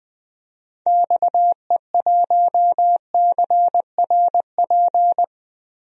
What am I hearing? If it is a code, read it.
Morse code, heard as a keyed tone, about 20 wpm, 700 Hz. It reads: XE1CRP
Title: DXCC-100 Call Signs - 6 Characters - Repeated 3x - 20wpm